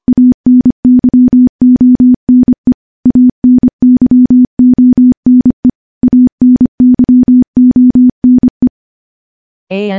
Morse code, heard as a keyed tone, about 25 wpm, 250 Hz. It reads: ANYONE ANYONE ANYONE
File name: morse / morse-training-25WPM